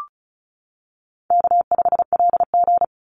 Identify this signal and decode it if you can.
Morse code, heard as a keyed tone, about 34 wpm, 700 Hz. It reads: K5LG
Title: Call Signs - 35wpm